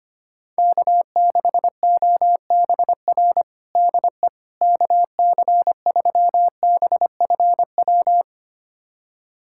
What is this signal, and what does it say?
Morse code, heard as a keyed tone, about 25 wpm, 700 Hz. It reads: K6OBR DE KC3BFW
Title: USA Callsign DE Callsign 25wpm